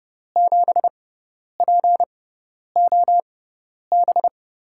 Morse code, heard as a keyed tone, about 30 wpm, spaced as for 10 wpm, 700 Hz. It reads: 7POB
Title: USA Call Signs - 6x Character Spacing - 30wpm